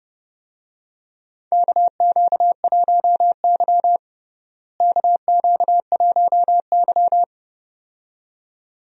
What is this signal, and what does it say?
Morse code, heard as a keyed tone, about 30 wpm, 700 Hz. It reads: KQ1Y KQ1Y
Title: USA Call Signs - 4 Characters - Repeated 2x - 30wpm